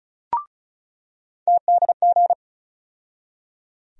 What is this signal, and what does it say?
Morse code, heard as a keyed tone, about 34 wpm, 700 Hz. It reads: TDG
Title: USA Call Sign Postfixes - 3 Letters 35wpm